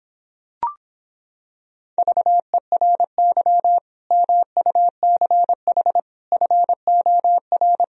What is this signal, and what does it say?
Morse code, heard as a keyed tone, about 26 wpm, 700 Hz. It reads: VERY MUCH FOR